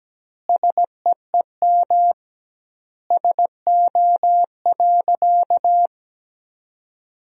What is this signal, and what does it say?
Morse code, heard as a keyed tone, about 17 wpm, 700 Hz. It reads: SEEM SO.
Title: 2x Word-Spacing - Sentences from Top 300 Words - 17wpm